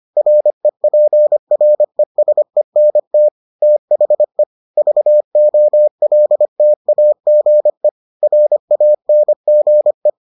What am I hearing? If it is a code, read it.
Morse code, heard as a keyed tone, about 25 wpm, 600 Hz. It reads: REPRESENT THE VOLTAGE RANGE